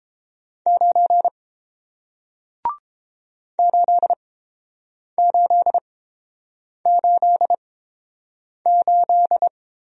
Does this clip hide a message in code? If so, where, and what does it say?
Yes: Morse code from 0.7 to 9.5 s, heard as a keyed tone, about 31 wpm, spaced as for 7 wpm, 700 Hz. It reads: 9 8888